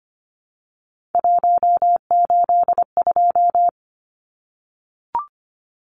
Morse code, heard as a keyed tone, about 25 wpm, 700 Hz. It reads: 182
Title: Numbers - 3-Digit 25wpm